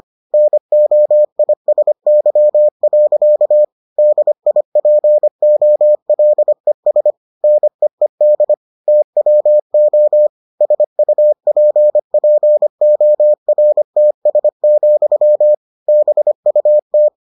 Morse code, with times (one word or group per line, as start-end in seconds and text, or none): 0.3-3.6 NOISY.
4.0-7.1 DIPOLES
7.4-8.5 NEED
8.9-10.3 TWO
10.6-15.5 SUPPORTS,
15.9-17.1 BUT